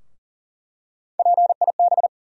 Morse code, heard as a keyed tone, about 40 wpm, 700 Hz. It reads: PIB
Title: USA Call Sign Postfixes - 3 Letters 40wpm